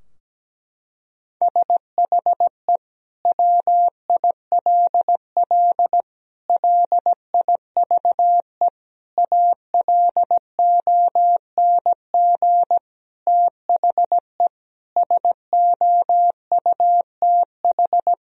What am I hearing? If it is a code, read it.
Morse code, heard as a keyed tone, about 17 wpm, 700 Hz. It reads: SHE WILL LIVE ALONG THE SOUTH
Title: Sentences from Top 1000 Words - Review Part 4 - 17wpm